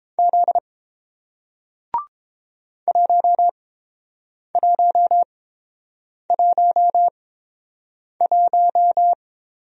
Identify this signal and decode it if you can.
Morse code, heard as a keyed tone, about 30 wpm, spaced as for 7 wpm, 690 Hz. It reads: Z 1111